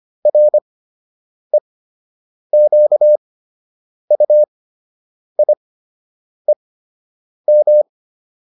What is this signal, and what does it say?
Morse code, heard as a keyed tone, about 25 wpm, spaced as for 8 wpm, 600 Hz. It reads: REQUIEM